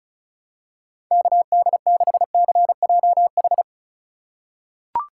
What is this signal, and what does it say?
Morse code, heard as a keyed tone, about 36 wpm, 690 Hz. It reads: KD6CJH
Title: USA Call Signs - 6 Characters - 3x Character Spacing - 35wpm